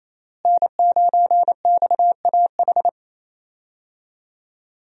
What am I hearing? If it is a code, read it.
Morse code, heard as a keyed tone, about 28 wpm, 700 Hz. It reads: N9XAH
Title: USA Call Signs - 5 Characters - Repeated 2x - 28wpm